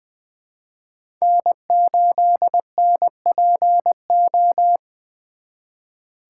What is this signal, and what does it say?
Morse code, heard as a keyed tone, about 22 wpm, 700 Hz. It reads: N8NPO
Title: Speed Racing - USA 1x3 Call Signs 20wpm